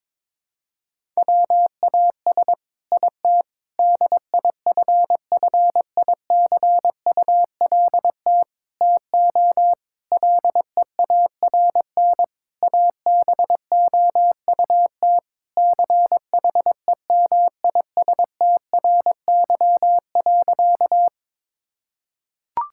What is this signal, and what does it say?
Morse code, heard as a keyed tone, about 22 wpm, 700 Hz. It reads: WAS IT DIFFICULT TO LEARN ABOUT CHEMISTRY.
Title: Sentences from Top 1000 Words - Review Part 6 - 22wpm